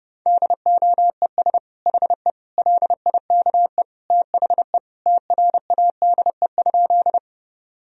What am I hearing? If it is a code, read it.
Morse code, heard as a keyed tone, about 30 wpm, 700 Hz. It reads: DOES HE LIKE THE TRADE?
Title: Sentences from Top 1000 Words - Review Part 4 - 30wpm